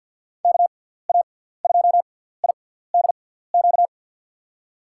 Morse code, heard as a keyed tone, about 47 wpm, 700 Hz. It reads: K A 2 I D Q
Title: USA Call Signs - 6 Characters - 6x Character Spacing - 50wpm